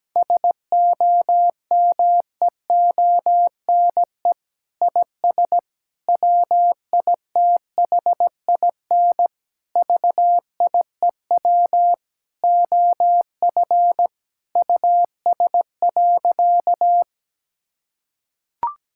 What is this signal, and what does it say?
Morse code, heard as a keyed tone, about 17 wpm, 700 Hz. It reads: SOMEONE IS WITHIN VIEW OF US.
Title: Sentences from Top 1000 Words - Review Part 6 - 17wpm